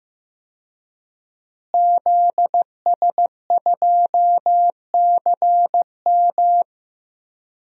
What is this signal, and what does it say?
Morse code, heard as a keyed tone, about 15 wpm, 700 Hz. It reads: ZS2CM
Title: DXCC-100 Call Signs - 5 Characters - Repeated 2x - 15wpm